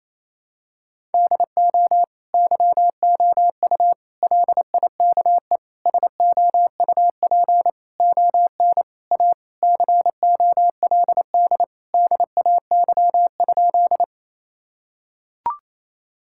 Morse code, heard as a keyed tone, about 27 wpm, 700 Hz. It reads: DO YOU LIKE SOUP ON A COLD DAY?
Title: Sentences from Top 1000 Words - Review Part 6 - 28wpm